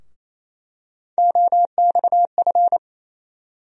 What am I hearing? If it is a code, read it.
Morse code, heard as a keyed tone, about 28 wpm, 700 Hz. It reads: OXF
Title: USA Call Sign Postfixes - 3 Letters 28wpm